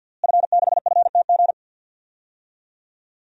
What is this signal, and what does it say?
Morse code, heard as a keyed tone, about 53 wpm, 700 Hz. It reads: F6FTG
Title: DXCC-100 Call Signs - 5 Characters - Repeated 2x - 50wpm